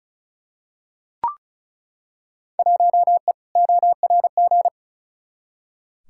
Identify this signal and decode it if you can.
Morse code, heard as a keyed tone, about 36 wpm, 700 Hz. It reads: 1E ORG
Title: Field Day Exchanges - 35wpm